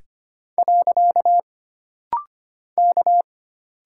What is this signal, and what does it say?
Morse code, heard as a keyed tone, about 25 wpm, 700 Hz. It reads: . K
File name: Single Character (Letter:Number:Punctuation:Prosign) - Rapid-Fire - 25wpm